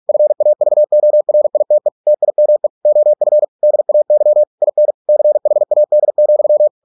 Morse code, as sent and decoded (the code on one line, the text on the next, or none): ..-. .- ...- --- .-. .. - . / - .. -- . / --- ..-. / -.. .- -.-- / .. -. / -.-. .... .- -.. --..--
FAVORITE TIME OF DAY IN CHAD,